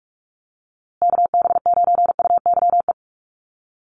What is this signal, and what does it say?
Morse code, heard as a keyed tone, about 44 wpm, 690 Hz. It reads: KB9UYE
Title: USA Call Signs Repeated 3x 45wpm